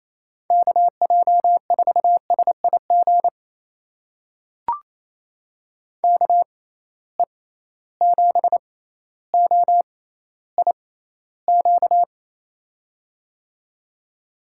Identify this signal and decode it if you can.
Morse code, heard as a keyed tone, about 28 wpm, spaced as for 14 wpm, 700 Hz. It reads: KJ4SIG KE7OIQ
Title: USA Call Signs - 6 Characters - 6x Character Spacing - 28wpm